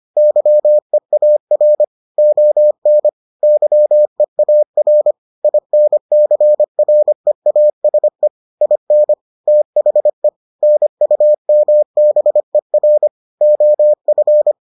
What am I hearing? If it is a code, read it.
Morse code, heard as a keyed tone, about 25 wpm, 600 Hz. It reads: YEAR ON YEAR INCREASE IN THE NUMBER OF